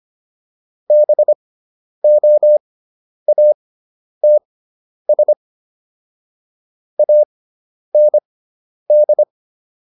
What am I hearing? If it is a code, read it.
Morse code, heard as a keyed tone, about 25 wpm, spaced as for 10 wpm, 600 Hz. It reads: BOATS AND
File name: SS213 10